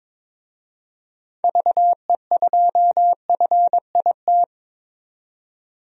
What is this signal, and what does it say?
Morse code, heard as a keyed tone, about 22 wpm, 700 Hz. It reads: VE2FIT
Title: DXCC-100 Call Signs - 6 Characters - 22wpm